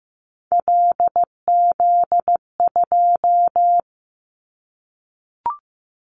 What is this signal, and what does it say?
Morse code, heard as a keyed tone, about 15 wpm, 700 Hz. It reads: LZ2
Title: Callsign Prefixes for Top 50 DXCC - 15wpm